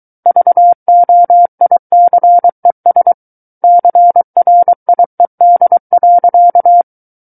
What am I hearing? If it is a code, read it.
Morse code, heard as a keyed tone, about 23 wpm, 700 Hz. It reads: VOICES CRIED.
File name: Book One / 14-Chapter 14 - 23 WPM